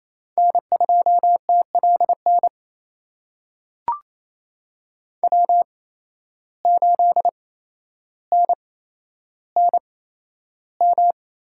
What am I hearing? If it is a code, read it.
Morse code, heard as a keyed tone, about 28 wpm, spaced as for 12 wpm, 700 Hz. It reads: N2TLN W8NNM
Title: USA Call Signs - 5 Characters - 8x Character Spacing - 28wpm